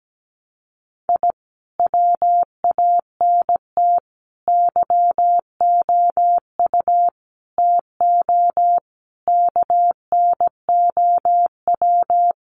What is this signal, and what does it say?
Morse code, heard as a keyed tone, about 17 wpm, 700 Hz. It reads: I WANT YOU TO KNOW
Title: Sets of 5 Words Encore 17wpm